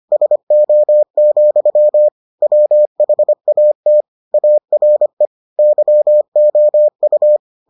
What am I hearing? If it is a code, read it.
Morse code, heard as a keyed tone, about 25 wpm, 600 Hz. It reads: SO, WHAT ARE YOU